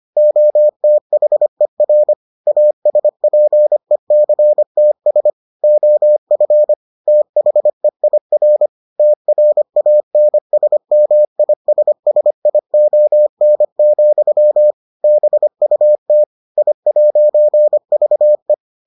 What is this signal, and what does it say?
Morse code, heard as a keyed tone, about 25 wpm, 600 Hz. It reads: OTHER ASPECTS OF THEIR TRANSMISSION, BUT I'VE